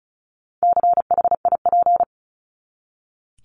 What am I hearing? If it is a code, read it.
Morse code, heard as a keyed tone, about 34 wpm, 700 Hz. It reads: CHIP
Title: Top 1751-2000 Words 35wpm